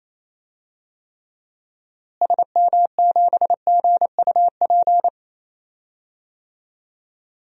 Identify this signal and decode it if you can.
Morse code, heard as a keyed tone, about 28 wpm, 700 Hz. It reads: SM7GUP
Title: DXCC-100 Call Signs - 6 Characters - Repeated 2x - 28wpm